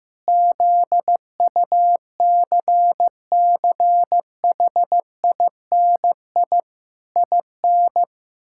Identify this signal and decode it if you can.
Morse code, heard as a keyed tone, about 15 wpm, 700 Hz. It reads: ZUCCHINI IN
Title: Pangram Challenge v2 - 15wpm